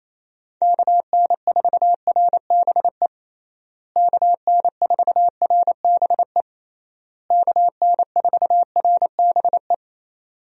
Morse code, heard as a keyed tone, about 28 wpm, 700 Hz. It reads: KN4RBE KN4RBE KN4RBE
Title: USA Call Signs - 6 Characters - Repeated 3x - 28wpm